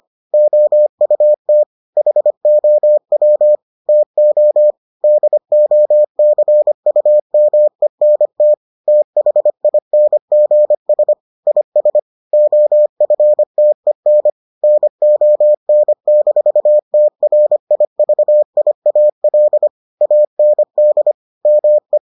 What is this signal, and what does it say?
Morse code, heard as a keyed tone, about 25 wpm, 600 Hz. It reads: OUT HOW TO DOCUMENT THINGS IS OFTEN NON-TRIVIAL AND ME